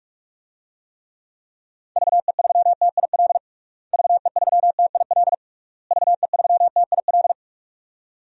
Morse code, heard as a keyed tone, about 46 wpm, 700 Hz. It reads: VE3TIL VE3TIL VE3TIL
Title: DXCC-100 Call Signs - 6 Characters - Repeated 3x - 45wpm